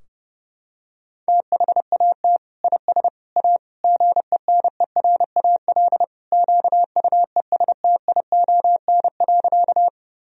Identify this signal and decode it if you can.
Morse code, heard as a keyed tone, about 32 wpm, 700 Hz. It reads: THAT IS A GENERAL QUESTION.
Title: Sentences from Top 1000 Words - Review Part 3 - 30wpm